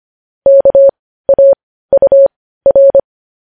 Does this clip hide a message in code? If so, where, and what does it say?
Yes: Morse code from 0.5 to 3.0 s, heard as a keyed tone, about 25 wpm, spaced as for 15 wpm, 550 Hz. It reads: KAUR